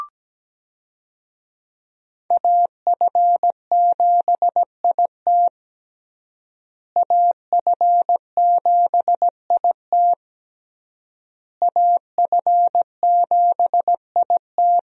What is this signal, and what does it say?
Morse code, heard as a keyed tone, about 17 wpm, 700 Hz. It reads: AF7IT AF7IT AF7IT
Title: USA Call Signs - 5 Characters - Repeated 3x - 17wpm